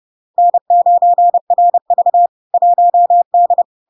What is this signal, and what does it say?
Morse code, heard as a keyed tone, about 30 wpm, 700 Hz. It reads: N9RV 1D
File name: LARK-FD-2022-Log-in-CW-30wpm-700-Hz